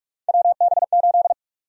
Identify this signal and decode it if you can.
Morse code, heard as a keyed tone, about 46 wpm, 700 Hz. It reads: WB8
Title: USA Call Sign Prefixes - 45wpm